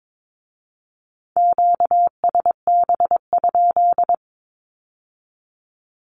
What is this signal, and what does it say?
Morse code, heard as a keyed tone, about 22 wpm, 700 Hz. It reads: QSB?
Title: Q-Codes with Spoken Meaning - 22wpm